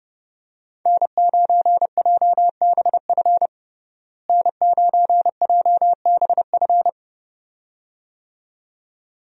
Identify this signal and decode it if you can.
Morse code, heard as a keyed tone, about 30 wpm, 700 Hz. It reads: N9JBF N9JBF
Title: USA Call Signs - 5 Characters - Repeated 2x - 30wpm